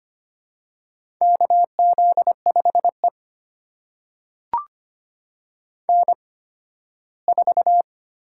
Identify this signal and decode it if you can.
Morse code, heard as a keyed tone, about 25 wpm, spaced as for 15 wpm, 700 Hz. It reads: KZ5E N4
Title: USA Call Signs - 4 Characters - 8x Character Spacing - 25wpm